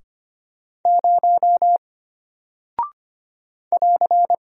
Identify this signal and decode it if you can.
Morse code, heard as a keyed tone, about 25 wpm, 700 Hz. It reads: 0 <AR>
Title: Single Character (Letter:Number:Punctuation:Prosign) Alt 25wpm